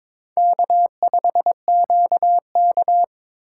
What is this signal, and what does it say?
Morse code, heard as a keyed tone, about 21 wpm, 700 Hz. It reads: K5QK
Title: USA Call Signs - 4 Characters - 4x Character Spacing - 22wpm